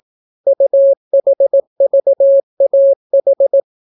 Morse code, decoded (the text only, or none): UHVAH